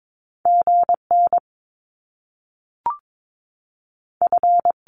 Morse code, heard as a keyed tone, about 22 wpm, 700 Hz. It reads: G N F
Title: Top Two Letter Combinations in Words 22wpm